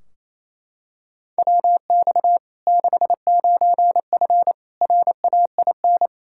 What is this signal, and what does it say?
Morse code, heard as a keyed tone, about 28 wpm, 700 Hz. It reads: WX 69F RAIN
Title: Practice QSO - Exchange 2 - Info - 28wpm